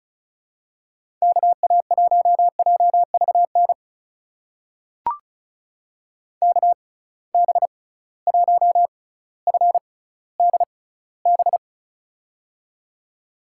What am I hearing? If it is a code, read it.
Morse code, heard as a keyed tone, about 35 wpm, spaced as for 18 wpm, 700 Hz. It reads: KA1JVN KB1FDB
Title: USA Call Signs - 6 Characters - 6x Character Spacing - 35wpm